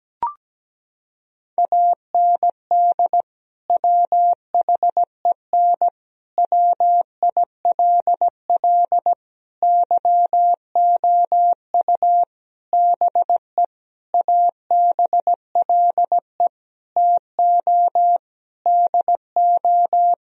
Morse code, heard as a keyed tone, about 17 wpm, 700 Hz. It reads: AND WHEN WILL YOU BE ABLE TO DO
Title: Sentences from Top 1000 Words - Review Part 2 - 17wpm